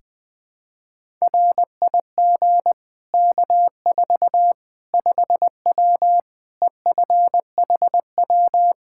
Morse code, heard as a keyed tone, about 20 wpm, 700 Hz. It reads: RIG K4 5W EFHW
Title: Practice QSO - Exchange 2 - Info - 20wpm